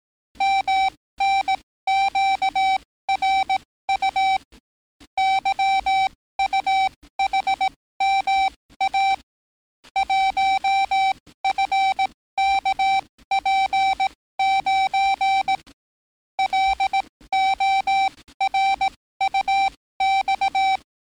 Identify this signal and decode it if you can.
Morse code, heard as a keyed tone, about 18 wpm, spaced as for 15 wpm, 780 Hz. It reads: MNQRU YUHMA 1FKP9 LORUX
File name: training / NAVPERS 11049to11060 / NAVPERS 11056 RA RB